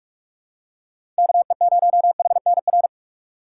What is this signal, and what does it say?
Morse code, heard as a keyed tone, about 46 wpm, 700 Hz. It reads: KE0HNR
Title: USA Call Signs - 2x Character Spacing - 45wpm